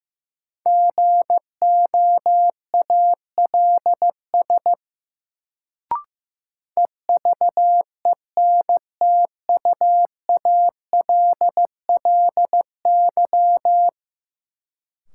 Morse code, heard as a keyed tone, about 15 wpm, 700 Hz. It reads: GOALS EVENTUALLY
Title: Sentences from Top 1000 Words - Review Part 6 - New Word Drill - 15wpm